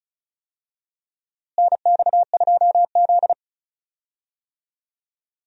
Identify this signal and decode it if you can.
Morse code, heard as a keyed tone, about 36 wpm, 700 Hz. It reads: NX2Z
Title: USA Call Signs - 4 Characters - Repeated 3x - 35wpm